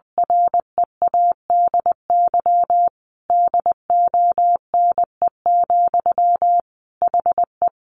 Morse code, heard as a keyed tone, about 20 wpm, 700 Hz. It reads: READY DONE, HE
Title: Sentences from Top 700 Words 20wpm